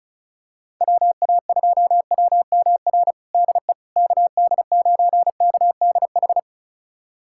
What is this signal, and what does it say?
Morse code, heard as a keyed tone, about 36 wpm, 690 Hz. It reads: WA2WMR DE KD9KDH
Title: USA Callsign DE Callsign - 6 Characters - 35wpm